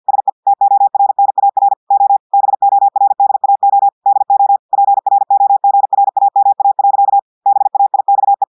Morse code, heard as a keyed tone, about 50 wpm, 800 Hz. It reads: SE TORNAR O BORDAO DO PROGRAMA. BAIXE
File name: portugais texte en clair 2 vitesse 50 mots minutes